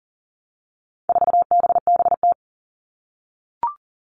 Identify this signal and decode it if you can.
Morse code, heard as a keyed tone, about 40 wpm, 700 Hz. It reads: 4BBT
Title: All Cut Numbers - 4-Digit 40wpm